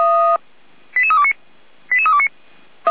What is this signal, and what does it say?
Five-tone selective call (ZVEI), heard as quick runs of five tones, tones of about 70 ms. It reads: 88329 88329